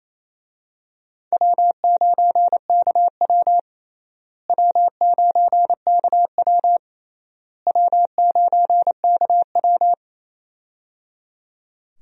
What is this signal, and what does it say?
Morse code, heard as a keyed tone, about 28 wpm, 700 Hz. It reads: W9KW W9KW W9KW
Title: USA Call Signs - 4 Characters - Repeated 3x - 28wpm